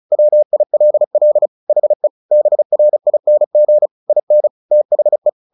Morse code, heard as a keyed tone, about 34 wpm, 600 Hz. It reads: WILL HE BRING IN THE
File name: the old man and the sea / the old man and the sea 1 35wpm